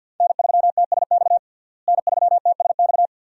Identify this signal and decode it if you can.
Morse code, heard as a keyed tone, about 47 wpm, 700 Hz. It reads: N3TSX N3TSX
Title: USA Call Signs - 5 Characters - Repeated 2x - 50wpm